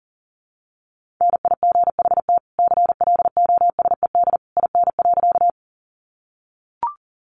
Morse code, heard as a keyed tone, about 40 wpm, 700 Hz. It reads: NIGHT CLOSED IN.
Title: Sentences from Top 1400 Words 40wpm